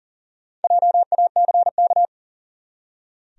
Morse code, heard as a keyed tone, about 40 wpm, 700 Hz. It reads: JACK
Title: USA Common Names - Max 5 Letters - 40wpm